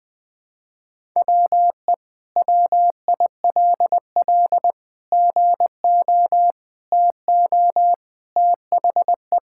Morse code, heard as a keyed tone, about 21 wpm, 700 Hz. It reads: WE WILL GO TO THE
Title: Sentences with 2-Word Phrases - 20wpm